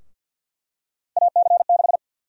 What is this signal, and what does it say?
Morse code, heard as a keyed tone, about 52 wpm, 700 Hz. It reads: AC6
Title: USA Call Sign Prefixes - Repeated 2x - 50wpm